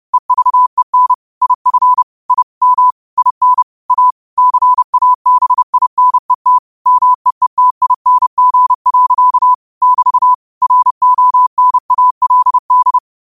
Morse code, heard as a keyed tone, about 30 wpm, 1000 Hz. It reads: EVEN IF I M IN A CABINET MEETING. <BT> RONALD
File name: qotdfiles / 30wpm